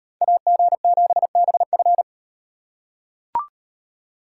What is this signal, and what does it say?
Morse code, heard as a keyed tone, about 36 wpm, 700 Hz. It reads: AG7BF